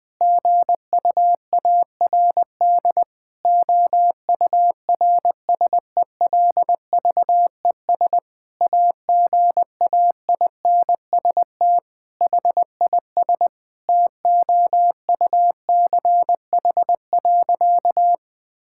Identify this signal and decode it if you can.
Morse code, heard as a keyed tone, about 20 wpm, 700 Hz. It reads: GUARD OURSELVES AGAINST HIS TOUCH.
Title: Sentences from Top 1600 Words 20wpm